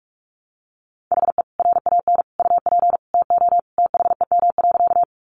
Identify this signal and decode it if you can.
Morse code, heard as a keyed tone, about 44 wpm, 700 Hz. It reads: HE RAN UP TO THEM.
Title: Sentences from Top 500 Words 45wpm